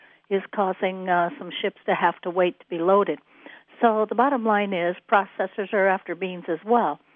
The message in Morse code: none

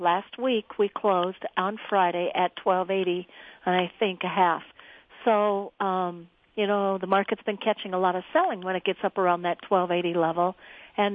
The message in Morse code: none

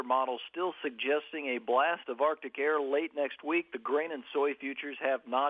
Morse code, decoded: none

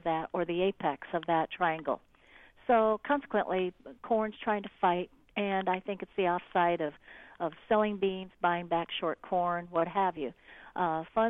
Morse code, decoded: none

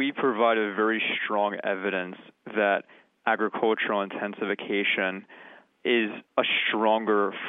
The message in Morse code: none

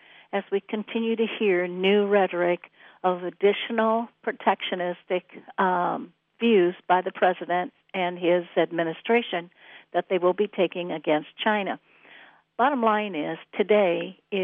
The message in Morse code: none

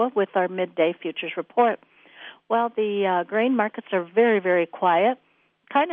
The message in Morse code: none

